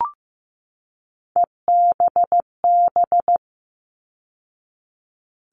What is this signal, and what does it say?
Morse code, heard as a keyed tone, about 15 wpm, 700 Hz. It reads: EBB
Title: All Cut Numbers - 3-Digit 15wpm